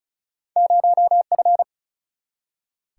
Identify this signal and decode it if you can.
Morse code, heard as a keyed tone, about 36 wpm, 700 Hz. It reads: 0F
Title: Single-Letter Single-Number 35wpm